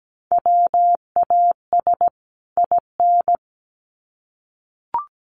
Morse code, heard as a keyed tone, about 17 wpm, 700 Hz. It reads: WAS IN
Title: Sets of 2 Words 17wpm